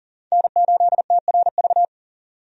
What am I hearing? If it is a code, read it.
Morse code, heard as a keyed tone, about 40 wpm, 700 Hz. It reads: N8TRV
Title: USA Call Signs - 5 Characters - 40wpm